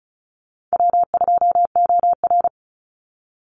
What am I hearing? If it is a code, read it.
Morse code, heard as a keyed tone, about 34 wpm, 700 Hz. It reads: W2OR
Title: Easy Call Signs Repeated 2x - 35wpm